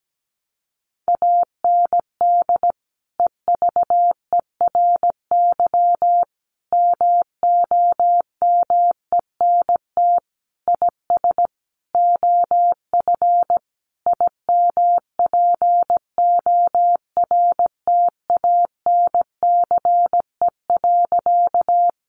Morse code, heard as a keyed tone, about 17 wpm, 700 Hz. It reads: AND EVERY MOMENT IS OF IMPORTANCE.
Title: Sentences from Top 1200 Words 17wpm